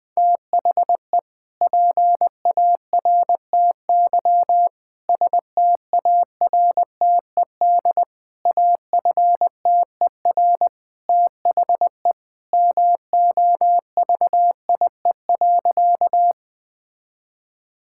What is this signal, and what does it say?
Morse code, heard as a keyed tone, about 21 wpm, 700 Hz. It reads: THE PARTY STARTED AFTER THE MOVIE.
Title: Sentences with 2-Word Phrases - Spoken Beginning - 20wpm